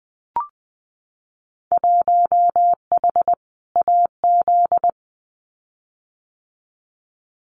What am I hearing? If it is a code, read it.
Morse code, heard as a keyed tone, about 21 wpm, 700 Hz. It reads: 1H AZ
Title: Winter Field Day Exchanges 20wpm